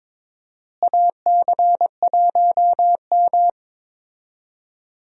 Morse code, heard as a keyed tone, about 22 wpm, 700 Hz. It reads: AC1M
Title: USA Call Signs - 4 Characters - Repeated 3x - 22wpm